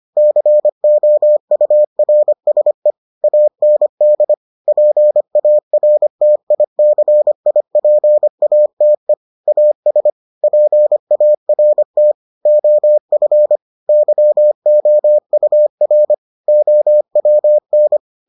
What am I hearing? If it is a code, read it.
Morse code, heard as a keyed tone, about 25 wpm, 600 Hz. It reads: COURSE AND PARTICIPATE AS PART OF YOUR OWN